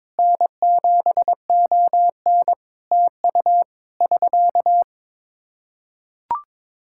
Morse code, heard as a keyed tone, about 22 wpm, 700 Hz. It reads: N7ON TU <SK>
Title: Practice QSO - Exchange 4 - Ending - 22wpm